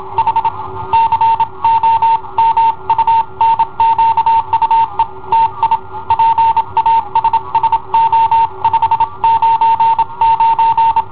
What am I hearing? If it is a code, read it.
Morse code, heard as a keyed tone, about 27 wpm, 910 Hz. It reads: H COMUNQUE TI PASSO 599
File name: i7ohp / cassetta i7ohp B